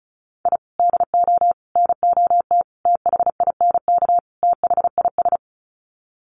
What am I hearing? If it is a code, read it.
Morse code, heard as a keyed tone, about 34 wpm, 700 Hz. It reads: I DO NOT THINK THIS
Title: Sets of 5 Words Encore 35wpm